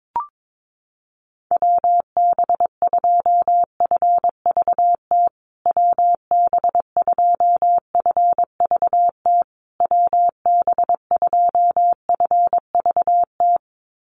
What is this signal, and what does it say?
Morse code, heard as a keyed tone, about 22 wpm, 690 Hz. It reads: WB2FVT WB2FVT WB2FVT
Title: USA Call Signs Repeated 3x 22wpm